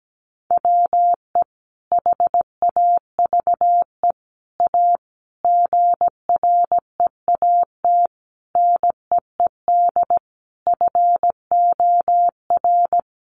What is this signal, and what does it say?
Morse code, heard as a keyed tone, about 17 wpm, 700 Hz. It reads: WE HAVE A GREAT NEED FOR
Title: Sentences from Top 1500 Words 17wpm